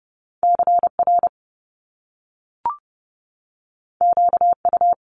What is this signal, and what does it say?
Morse code, heard as a keyed tone, about 30 wpm, 700 Hz. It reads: CR QU
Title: Top Two Letter Combinations in Words 30wpm